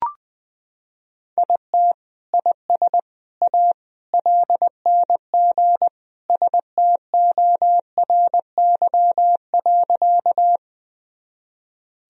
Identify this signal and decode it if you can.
Morse code, heard as a keyed tone, about 20 wpm, 700 Hz. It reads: IT IS A LNG STORY.
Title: Sentences from Top 300 Words and Abbreviations 20wpm